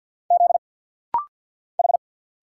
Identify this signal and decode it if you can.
Morse code, heard as a keyed tone, about 50 wpm, 700 Hz. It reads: Z H
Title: Single Letter-Number Alt - Rapid-Fire 50wpm